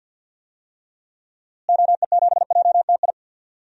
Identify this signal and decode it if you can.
Morse code, heard as a keyed tone, about 52 wpm, 700 Hz. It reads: OE7JTI
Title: DXCC-100 Call Signs - 6 Characters - Repeated 3x - 50wpm